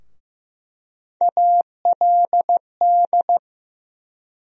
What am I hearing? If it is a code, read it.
Morse code, heard as a keyed tone, about 15 wpm, 700 Hz. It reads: ALD
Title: USA Call Sign Postfixes - 3 Letters 15wpm